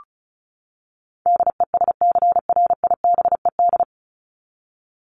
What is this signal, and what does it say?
Morse code, heard as a keyed tone, about 34 wpm, 700 Hz. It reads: DESCRIBED
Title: ED Suffix - Top 100 Words with Base Words 35wpm